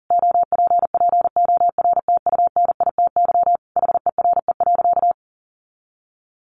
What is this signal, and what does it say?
Morse code, heard as a keyed tone, about 40 wpm, 700 Hz. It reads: OPPORTUNITY HERE.
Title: Sentences from Top 1300 Words 40wpm